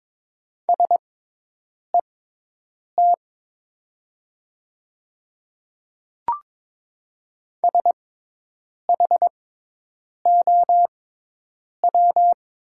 Morse code, heard as a keyed tone, about 22 wpm, spaced as for 8 wpm, 700 Hz. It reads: SET SHOW
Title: 6x Character Spacing - Top 200 Words - 22wpm